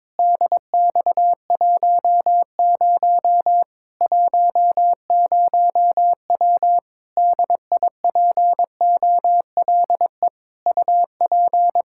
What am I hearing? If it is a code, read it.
Morse code, heard as a keyed tone, about 22 wpm, 700 Hz. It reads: DX10 10W DIPOLE UP
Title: Practice QSO - Exchange 2 - Info - 22wpm